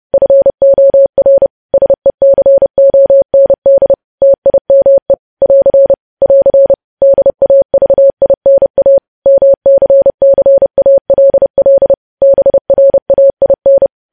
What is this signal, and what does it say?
Morse code, heard as a keyed tone, about 30 wpm, 550 Hz. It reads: FOR SECOND TIME <AR> <AR> DAVINA MCCALL BRAIN
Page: news headlines in morse code at 30 wpm